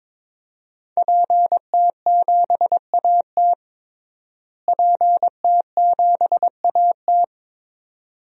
Morse code, heard as a keyed tone, about 22 wpm, 700 Hz. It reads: PT7AT PT7AT
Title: DXCC-100 Call Signs - 5 Characters - Repeated 2x - 22wpm